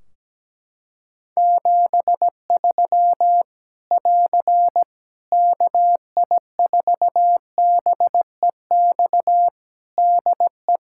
Morse code, heard as a keyed tone, about 17 wpm, 700 Hz. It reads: 73 <AR> KI4BEX DE
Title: Practice QSO - Exchange 4 - Ending - 17wpm